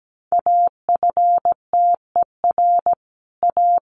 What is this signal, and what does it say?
Morse code, heard as a keyed tone, about 17 wpm, 700 Hz. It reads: AFTER A
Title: Sets of 2 Words 17wpm